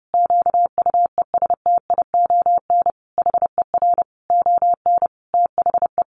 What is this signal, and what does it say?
Morse code, heard as a keyed tone, about 30 wpm, 700 Hz. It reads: QUESTION HER ON THE
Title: Sentences from Top 800 Words 30wpm